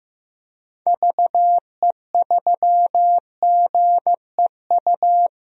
Morse code, heard as a keyed tone, about 16 wpm, 700 Hz. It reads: VE3GEU
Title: DXCC-100 Call Signs - 6 Characters - Repeated 3x - 15wpm